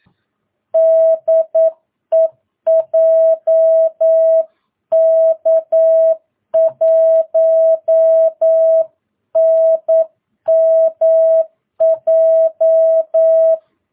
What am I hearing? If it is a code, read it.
Morse code, heard as a keyed tone, about 9 wpm, 650 Hz. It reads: DE JK1NMJ
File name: jh8cbh / 2022cw